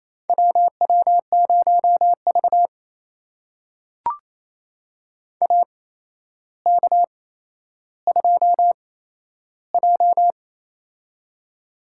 Morse code, heard as a keyed tone, about 28 wpm, spaced as for 12 wpm, 700 Hz. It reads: WW0V AK2J
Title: USA Call Signs - 4 Characters - 8x Character Spacing - 28wpm